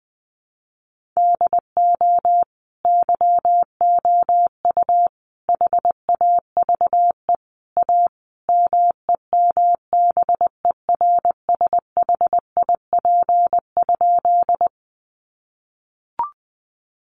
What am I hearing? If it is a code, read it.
Morse code, heard as a keyed tone, about 20 wpm, 700 Hz. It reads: DO YOU HAVE A MEMBERSHIP?
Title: Sentences from Top 1400 Words 20wpm